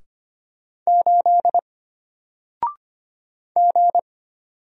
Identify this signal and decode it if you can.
Morse code, heard as a keyed tone, about 25 wpm, 700 Hz. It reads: 8 G
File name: Single-Letter-Number-25wpm